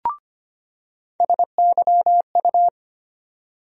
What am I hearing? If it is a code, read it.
Morse code, heard as a keyed tone, about 25 wpm, 700 Hz. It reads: SYU
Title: USA Call Sign Postfixes - 3 Letters 25wpm